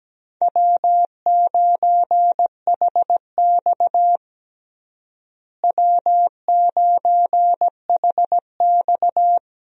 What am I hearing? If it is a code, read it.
Morse code, heard as a keyed tone, about 17 wpm, 700 Hz. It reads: W9HX W9HX
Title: USA Call Signs - 4 Characters - Repeated 2x - 17wpm